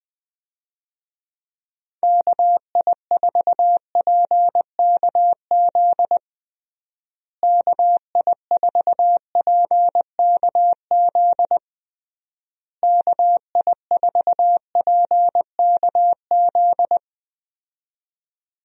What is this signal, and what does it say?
Morse code, heard as a keyed tone, about 21 wpm, 700 Hz. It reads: KI4PKZ KI4PKZ KI4PKZ
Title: USA Call Signs - 6 Characters - Repeated 3x - 20wpm